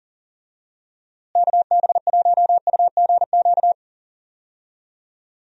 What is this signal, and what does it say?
Morse code, heard as a keyed tone, about 38 wpm, 700 Hz. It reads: KB1UGQ
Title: USA Call Signs - 6 Characters - Repeated 3x - 40wpm